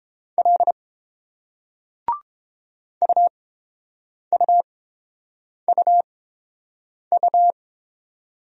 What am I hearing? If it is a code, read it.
Morse code, heard as a keyed tone, about 29 wpm, spaced as for 7 wpm, 700 Hz. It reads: L UUUU